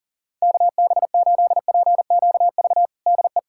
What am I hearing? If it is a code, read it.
Morse code, heard as a keyed tone, about 40 wpm, 700 Hz. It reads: KB8PQV DE
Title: USA Callsign DE Callsign - 6 Characters - 40wpm